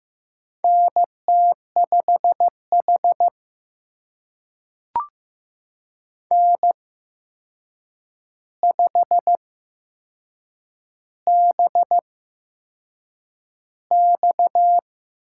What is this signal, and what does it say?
Morse code, heard as a keyed tone, about 15 wpm, spaced as for 6 wpm, 700 Hz. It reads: NT5H N5BX
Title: USA Call Signs - 4 Characters - 8x Character Spacing - 15wpm